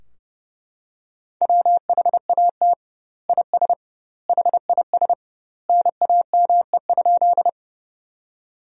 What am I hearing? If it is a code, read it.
Morse code, heard as a keyed tone, about 32 wpm, 700 Hz. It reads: WHAT IS HIS NAME?
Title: 2x Word-Spacing - Sentences from Top 200 Words - 30wpm